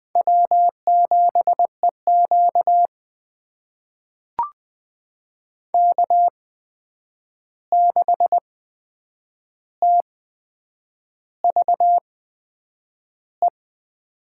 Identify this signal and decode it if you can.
Morse code, heard as a keyed tone, about 20 wpm, spaced as for 8 wpm, 700 Hz. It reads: W7EQ K6TVE
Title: USA Call Signs - 8x Character Spacing - 20wpm